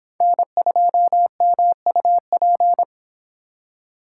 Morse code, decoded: N2MUP

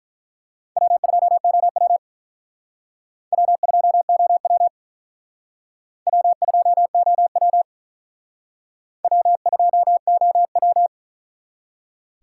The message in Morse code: W2OW W2OW W2OW W2OW

